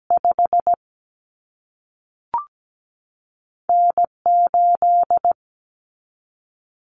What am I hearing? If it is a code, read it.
Morse code, heard as a keyed tone, about 17 wpm, spaced as for 3 wpm, 700 Hz. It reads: 5N 8
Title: Common Cut Numbers - 2-Digit 17wpm